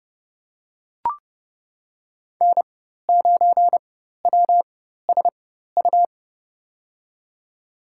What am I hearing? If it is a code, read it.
Morse code, heard as a keyed tone, about 30 wpm, spaced as for 14 wpm, 700 Hz. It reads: N9WSU
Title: USA Call Signs - 5 Characters - 4x Character Spacing - 30wpm